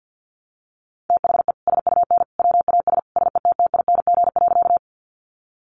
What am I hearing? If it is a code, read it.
Morse code, heard as a keyed tone, about 50 wpm, 700 Hz. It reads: THE SUN WAS SETTING.
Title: Sentences from Top 1300 Words 50wpm